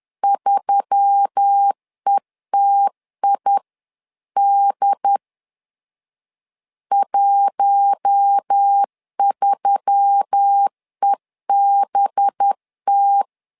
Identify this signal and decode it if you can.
Morse code, heard as a keyed tone, about 10 wpm, 800 Hz. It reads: 3ETID 13EBT
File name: LEZIONI / 5 LBVC / 5 LBV 10wpm 2248